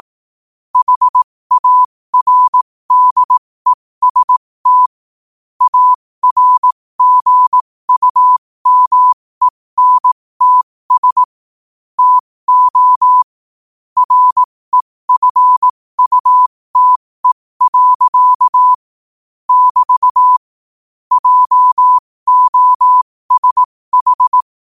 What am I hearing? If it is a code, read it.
Morse code, heard as a keyed tone, about 18 wpm, spaced as for 15 wpm, 1000 Hz. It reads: HARDEST ARGUMENTS TO REFUTE. <BT> JOSH
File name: 15wpm